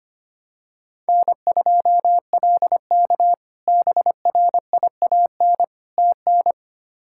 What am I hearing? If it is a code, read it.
Morse code, heard as a keyed tone, about 25 wpm, 700 Hz. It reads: N2LK BRIAN TN
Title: K1USN SST QSO Practice - 25wpm